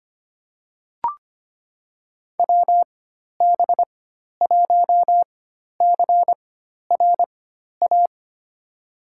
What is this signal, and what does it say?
Morse code, heard as a keyed tone, about 25 wpm, spaced as for 12 wpm, 700 Hz. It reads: WB1CRA
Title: USA Call Signs - 6 Characters - 4x Character Spacing - 25wpm